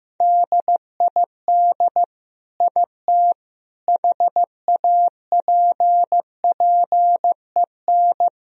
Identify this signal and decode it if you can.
Morse code, heard as a keyed tone, about 15 wpm, 700 Hz. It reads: DID IT HAPPEN
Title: Sentences from Top 1000 Words - Review Part 2 - 15wpm